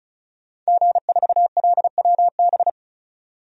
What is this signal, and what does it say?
Morse code, heard as a keyed tone, about 36 wpm, 700 Hz. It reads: G4LWB
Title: DXCC-100 Call Signs - 5 Characters - 35wpm